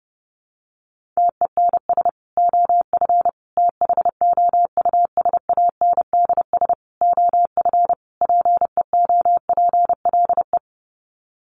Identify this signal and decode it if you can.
Morse code, heard as a keyed tone, about 30 wpm, 700 Hz. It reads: TENS OF THOUSANDS OF PEOPLE
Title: Sets of 5 Words Encore2 30wpm